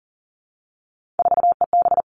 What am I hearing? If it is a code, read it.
Morse code, heard as a keyed tone, about 40 wpm, 700 Hz. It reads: 4EB
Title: All Cut Numbers - 3-Digit 40wpm